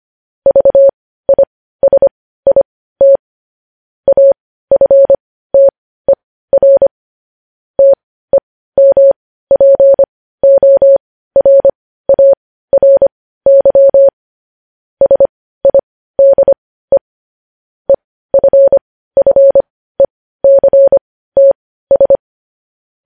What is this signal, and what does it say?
Morse code, heard as a keyed tone, about 25 wpm, spaced as for 15 wpm, 550 Hz. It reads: VISIT AFTER TEMPORARY SIDE EFFECTS
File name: podcast / News headlines-1x-25x15